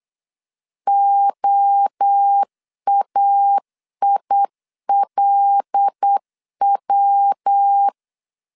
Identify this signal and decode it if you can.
Morse code, heard as a keyed tone, about 8 wpm, 800 Hz. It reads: OAILW
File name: LEZIONI / 6 PKX0 / 6 PKX0 8wpm 3341